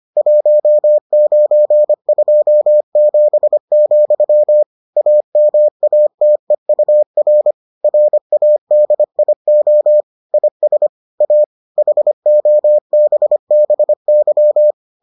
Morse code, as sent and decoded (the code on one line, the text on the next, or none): .---- ----. ..--- --... --..-- / .- -- .- - . ..- .-. / .-. .- -.. .. --- / .. ... / .- / .... --- -... -... -.--
1927, AMATEUR RADIO IS A HOBBY